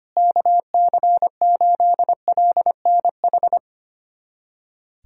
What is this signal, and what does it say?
Morse code, heard as a keyed tone, about 25 wpm, 700 Hz. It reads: KC8LNH
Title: USA Callsign DE Callsign 25wpm